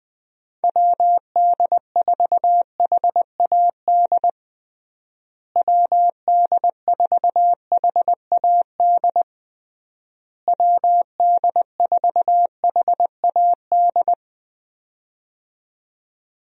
Morse code, heard as a keyed tone, about 21 wpm, 700 Hz. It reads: WD4HAD WD4HAD WD4HAD
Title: USA Call Signs - 6 Characters - Repeated 3x - 20wpm